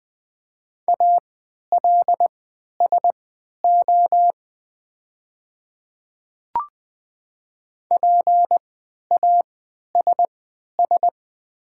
Morse code, heard as a keyed tone, about 20 wpm, spaced as for 11 wpm, 700 Hz. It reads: ALSO PASS
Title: 3x Character Spacing - Top 500 Words - 20wpm